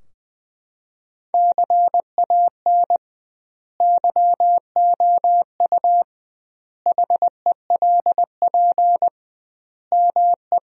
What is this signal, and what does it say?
Morse code, heard as a keyed tone, about 20 wpm, 700 Hz. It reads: CAN YOU HELP ME
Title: 2x Word-Spacing - Sentences from Top 200 Words - 20wpm